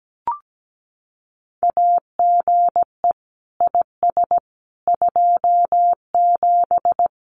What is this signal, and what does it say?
Morse code, heard as a keyed tone, about 17 wpm, 700 Hz. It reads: AGE IS 27
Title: Common QSO Phrases Encore 17wpm